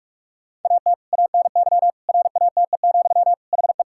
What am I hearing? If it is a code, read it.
Morse code, heard as a keyed tone, about 46 wpm, 690 Hz. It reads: AT ANY RATE, HE
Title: Sentences from Top 1000 Words - Review Part 5 - 45wpm